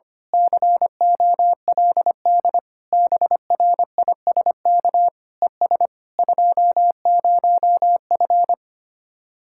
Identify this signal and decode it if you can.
Morse code, heard as a keyed tone, about 24 wpm, 700 Hz. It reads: COLD BRISK ES 20F
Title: Speed-Racing Common QSO Phrases Encore - 25wpm